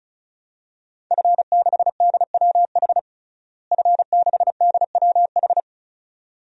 Morse code, heard as a keyed tone, about 34 wpm, 700 Hz. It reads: F6DWH F6DWH
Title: DXCC-100 Call Signs - 5 Characters - Repeated 2x - 35wpm